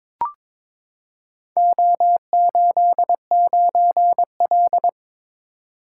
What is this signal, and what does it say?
Morse code, heard as a keyed tone, about 22 wpm, 700 Hz. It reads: O89L
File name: practice / 22wpm / Four Letter-Number Combinations - 22wpm